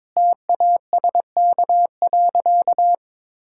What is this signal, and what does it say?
Morse code, heard as a keyed tone, about 22 wpm, 700 Hz. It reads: TASK.
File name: Difficult Sentences with Hints - 22wpm